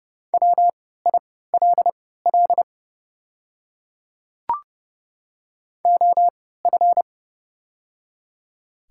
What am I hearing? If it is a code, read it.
Morse code, heard as a keyed tone, about 30 wpm, spaced as for 17 wpm, 700 Hz. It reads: WILL OF
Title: 3x Character Spacing - Top 100 Words - 30wpm